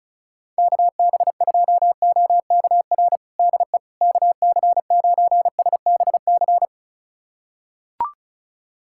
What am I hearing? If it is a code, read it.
Morse code, heard as a keyed tone, about 36 wpm, 700 Hz. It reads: KB2OKR DE KC9SBC
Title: USA Callsign DE Callsign - 6 Characters - 35wpm